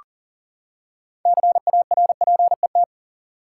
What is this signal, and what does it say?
Morse code, heard as a keyed tone, about 40 wpm, 700 Hz. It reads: CARPET
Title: Six Letter Words - 40wpm